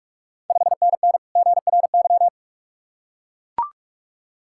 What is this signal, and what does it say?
Morse code, heard as a keyed tone, about 45 wpm, 690 Hz. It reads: 5NN GRY
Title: Ontario QSO Party Exchange 45wpm